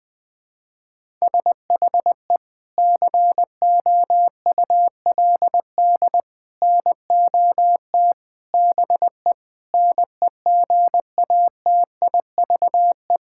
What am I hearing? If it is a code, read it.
Morse code, heard as a keyed tone, about 20 wpm, 700 Hz. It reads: SHE COULD NOT BE NEGATIVE
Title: Sentences from Top 1000 Words - Review Part 6 - 20wpm